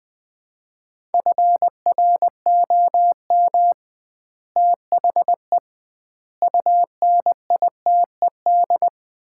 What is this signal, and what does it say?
Morse code, heard as a keyed tone, about 20 wpm, 700 Hz. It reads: FROM THE UNITED
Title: 2x Word-Spacing - Sets of 4 Words - 20wpm